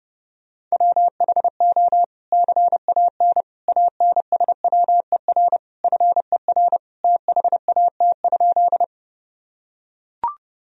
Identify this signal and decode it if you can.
Morse code, heard as a keyed tone, about 30 wpm, 700 Hz. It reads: WHO CAN ANSWER FER THAT?
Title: Sentences from Top 300 Words and Abbreviations 30wpm